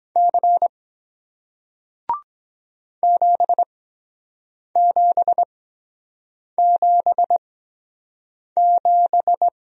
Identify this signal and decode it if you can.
Morse code, heard as a keyed tone, about 23 wpm, spaced as for 7 wpm, 700 Hz. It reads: C 7777